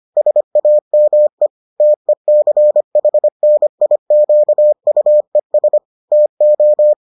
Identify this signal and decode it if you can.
Morse code, heard as a keyed tone, about 25 wpm, 600 Hz. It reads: SAME TECHNIQUES TO